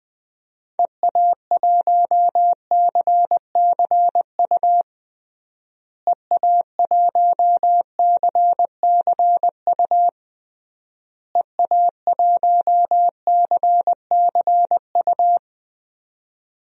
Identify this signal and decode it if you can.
Morse code, heard as a keyed tone, about 20 wpm, 700 Hz. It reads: EA1CCU EA1CCU EA1CCU
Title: DXCC-100 Call Signs - 6 Characters - Repeated 3x - 20wpm